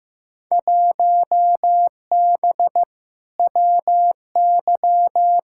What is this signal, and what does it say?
Morse code, heard as a keyed tone, about 15 wpm, 700 Hz. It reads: 1B WY
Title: Field Day Exchanges Encore - 15wpm